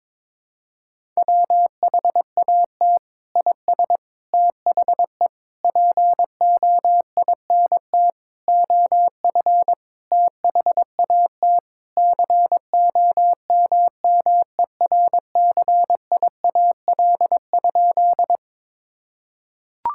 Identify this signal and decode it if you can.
Morse code, heard as a keyed tone, about 22 wpm, 700 Hz. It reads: WHAT IS THE POINT OF THAT COMMERCIAL?
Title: Sentences from Top 1000 Words - Review Part 5 - 22wpm